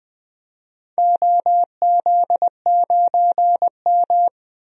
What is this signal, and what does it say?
Morse code, heard as a keyed tone, about 20 wpm, 700 Hz. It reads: OZ9M
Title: DXCC-100 Call Signs - 4 Characters - 20wpm